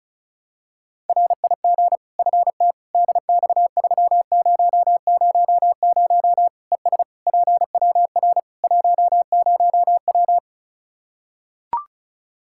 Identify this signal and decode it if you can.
Morse code, heard as a keyed tone, about 36 wpm, 700 Hz. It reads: RIG FT DX3000 ES PWR 10W
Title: Practice QSO - Exchange 2 - Info - 35wpm